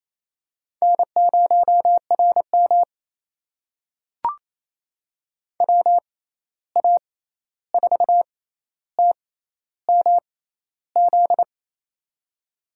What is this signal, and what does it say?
Morse code, heard as a keyed tone, about 28 wpm, spaced as for 13 wpm, 700 Hz. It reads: N0RM WA4TMZ
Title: USA Call Signs - 6x Character Spacing - 28wpm